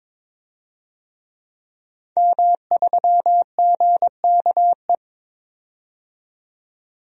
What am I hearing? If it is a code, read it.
Morse code, heard as a keyed tone, about 22 wpm, 700 Hz. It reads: M3GKE
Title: DXCC-100 Call Signs - 5 Characters - 22wpm